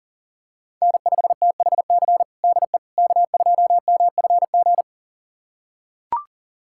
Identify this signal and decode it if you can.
Morse code, heard as a keyed tone, about 40 wpm, 700 Hz. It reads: N5THC DE K2MFG
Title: USA Callsign DE Callsign - 5 Characters - 40wpm